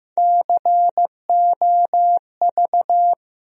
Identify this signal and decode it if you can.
Morse code, heard as a keyed tone, about 15 wpm, 700 Hz. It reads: COV
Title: USA Call Sign Postfixes - 3 Letters 15wpm